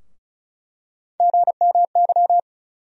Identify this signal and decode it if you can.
Morse code, heard as a keyed tone, about 36 wpm, 700 Hz. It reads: GMY
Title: USA Call Sign Postfixes - 3 Letters 35wpm